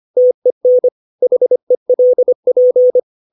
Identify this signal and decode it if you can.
Morse code, heard as a keyed tone, about 25 wpm, 500 Hz. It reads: TEN HELP
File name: 632 words / 632 Words 25WPM